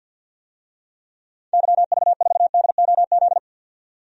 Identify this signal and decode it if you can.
Morse code, heard as a keyed tone, about 47 wpm, 700 Hz. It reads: YV4DOZ
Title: DXCC-100 Call Signs - 6 Characters - 50wpm